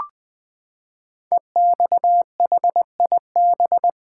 Morse code, heard as a keyed tone, about 20 wpm, 700 Hz. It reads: EXHIB
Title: Build A Word - 5 Letters At A Time - 10-Letter Words - 20wpm